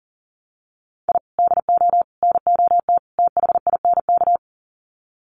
Morse code, heard as a keyed tone, about 40 wpm, 700 Hz. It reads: I DO NOT THINK
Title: Sets of 4 Words 40wpm